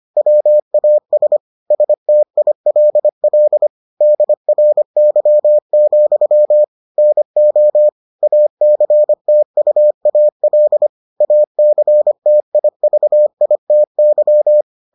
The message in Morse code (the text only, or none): WAS STILL DRY, NO ACTUAL ACTIVITY